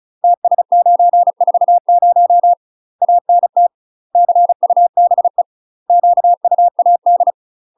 Morse code, heard as a keyed tone, about 36 wpm, 700 Hz. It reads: TS940 ANT CUBE QUAD
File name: UK MASTER QSOs TRAINING / 35WPM / UK MASTER QSOs 0036